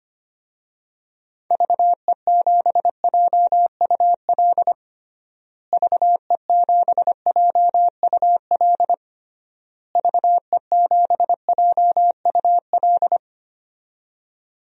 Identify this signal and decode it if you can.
Morse code, heard as a keyed tone, about 25 wpm, 700 Hz. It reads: VE7JUL VE7JUL VE7JUL
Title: DXCC-100 Call Signs - 6 Characters - Repeated 3x - 25wpm